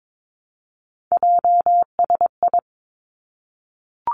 Morse code, heard as a keyed tone, about 22 wpm, 700 Hz. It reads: JSI
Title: Three Challenging Letter-Number Combinations 22wpm